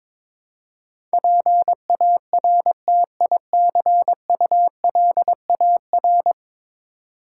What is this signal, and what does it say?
Morse code, heard as a keyed tone, about 22 wpm, 700 Hz. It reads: PARTICULAR
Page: morse-code-ninja-practice-22wpm